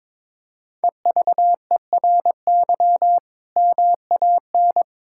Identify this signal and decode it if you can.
Morse code, heard as a keyed tone, about 22 wpm, 700 Hz. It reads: EVERY MAN
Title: Sentences from Top 1600 Words 22wpm